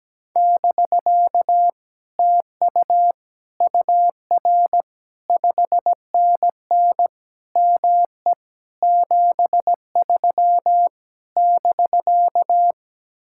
Morse code, heard as a keyed tone, about 17 wpm, 700 Hz. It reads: <BK> TU UR 5NN ME 73 <BK>
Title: Speed-Racing Common QSO Phrases Encore - 17wpm